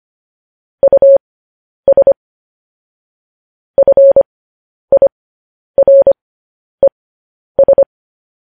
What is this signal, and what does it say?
Morse code, heard as a keyed tone, about 25 wpm, spaced as for 10 wpm, 550 Hz. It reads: US FIRES
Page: news headlines in morse code at 10 wpm